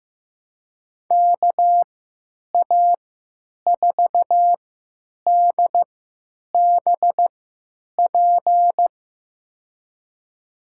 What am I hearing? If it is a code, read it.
Morse code, heard as a keyed tone, about 15 wpm, spaced as for 9 wpm, 700 Hz. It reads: KA4DBP
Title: USA Call Signs - 6 Characters - 3x Character Spacing - 15wpm